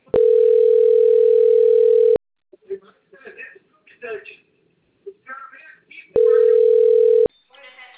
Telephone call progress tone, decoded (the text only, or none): ringback tone